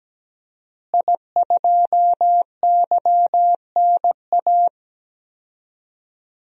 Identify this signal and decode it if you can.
Morse code, heard as a keyed tone, about 17 wpm, 700 Hz. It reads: I2YNA
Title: DXCC-100 Call Signs - 5 Characters - Repeated 2x - 17wpm